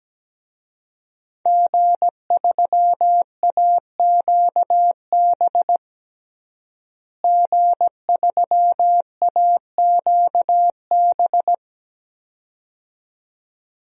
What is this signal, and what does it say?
Morse code, heard as a keyed tone, about 17 wpm, 700 Hz. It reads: G3AQB G3AQB
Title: DXCC-100 Call Signs - 5 Characters - Repeated 2x - 17wpm